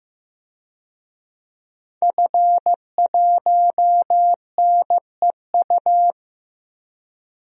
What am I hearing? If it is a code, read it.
Morse code, heard as a keyed tone, about 15 wpm, 700 Hz. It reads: F1NEU
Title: DXCC-100 Call Signs - 5 Characters - Repeated 2x - 15wpm